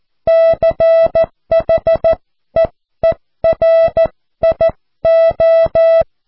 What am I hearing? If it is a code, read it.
Morse code, heard as a keyed tone, about 14 wpm, spaced as for 12 wpm, 660 Hz. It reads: CHEERIO
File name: cw / QSO